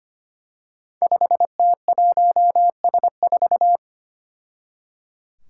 Morse code, heard as a keyed tone, about 25 wpm, 700 Hz. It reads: 5T1S4
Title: Five Challenging Letter-Number Combinations 25wpm